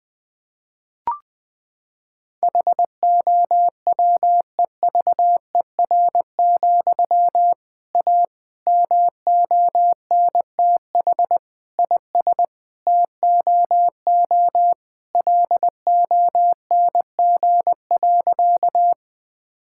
Morse code, heard as a keyed tone, about 20 wpm, 700 Hz. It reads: HOWEVER, A MONTH IS TOO LONG.
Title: Sentences from Top 1000 Words - Review Part 5 - 20wpm